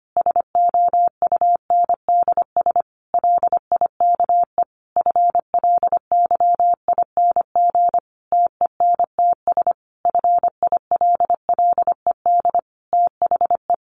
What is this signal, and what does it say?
Morse code, heard as a keyed tone, about 25 wpm, 700 Hz. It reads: SOUNDS LIKE FLYING TENTS FILLED THE
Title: A Sound Of Thunder 25wpm